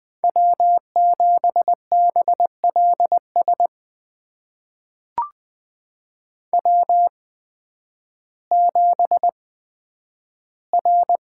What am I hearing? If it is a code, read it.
Morse code, heard as a keyed tone, about 20 wpm, spaced as for 11 wpm, 700 Hz. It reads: W7BLS W7R
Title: USA Call Signs - 8x Character Spacing - 20wpm